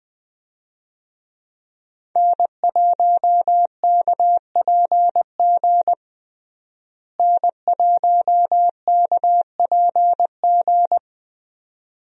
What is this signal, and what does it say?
Morse code, heard as a keyed tone, about 21 wpm, 700 Hz. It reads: N1KPG N1KPG
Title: USA Call Signs - 5 Characters - Repeated 2x - 20wpm